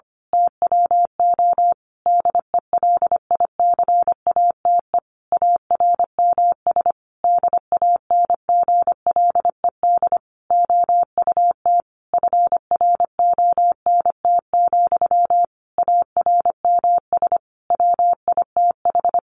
Morse code, heard as a keyed tone, about 25 wpm, 700 Hz. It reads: TWO DELICATE ARMS DANGLED OUT FRONT, ARMS WITH
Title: A Sound Of Thunder 25wpm